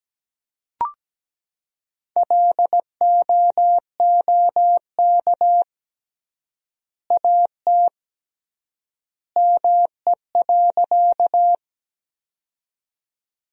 Morse code, heard as a keyed tone, about 17 wpm, 700 Hz. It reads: LOOK AT ME.
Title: 3x Word-Spacing - Sentences from Top 200 Words - 17wpm